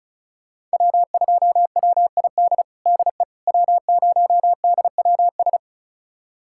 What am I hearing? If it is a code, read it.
Morse code, heard as a keyed tone, about 36 wpm, 700 Hz. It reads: W2WID DE W0DWS
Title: USA Callsign DE Callsign - 5 Characters - 35wpm